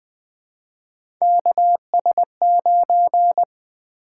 Morse code, heard as a keyed tone, about 20 wpm, 700 Hz. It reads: KS9
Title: Three Letter-Number Combinations - 20wpm